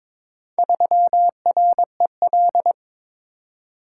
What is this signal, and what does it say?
Morse code, heard as a keyed tone, about 22 wpm, 700 Hz. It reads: 3REL
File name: practice / 22wpm / Four Letter-Number Combinations - 22wpm